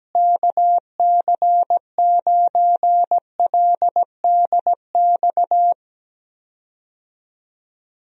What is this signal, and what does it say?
Morse code, heard as a keyed tone, about 17 wpm, 700 Hz. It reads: KC9LDX
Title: USA Call Signs - 6 Characters - Repeated 3x - 17wpm